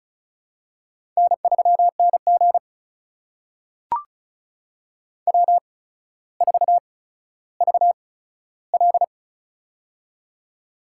Morse code, heard as a keyed tone, about 36 wpm, spaced as for 15 wpm, 700 Hz. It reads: N3NG W4VL
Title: USA Call Signs - 4 Characters - 8x Character Spacing - 35wpm